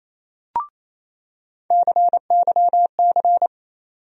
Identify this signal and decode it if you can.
Morse code, heard as a keyed tone, about 28 wpm, 700 Hz. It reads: CYC
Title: USA Call Sign Postfixes - 3 Letters 28wpm